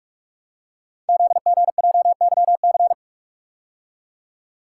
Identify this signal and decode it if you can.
Morse code, heard as a keyed tone, about 44 wpm, 700 Hz. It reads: ZGJYC
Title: Code Groups - 5 Letter:Numbers - 45wpm